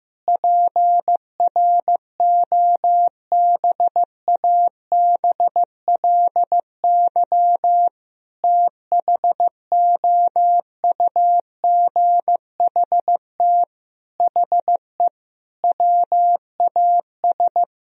Morse code, heard as a keyed tone, about 15 wpm, 700 Hz. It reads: PROBABLY THOUGHT HE WAS
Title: Sentences from Top 1000 Words - Review Part 4 - 15wpm